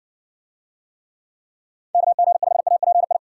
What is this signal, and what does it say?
Morse code, heard as a keyed tone, about 60 wpm, 700 Hz. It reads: KG5API